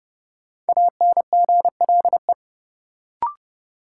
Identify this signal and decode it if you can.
Morse code, heard as a keyed tone, about 32 wpm, 700 Hz. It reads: ANGLE
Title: Sentences from Top 1500 Words - New Word Drill - 30wpm